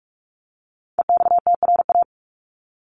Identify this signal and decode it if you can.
Morse code, heard as a keyed tone, about 44 wpm, 700 Hz. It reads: EXTRA
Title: Top 751-1000 Words 45wpm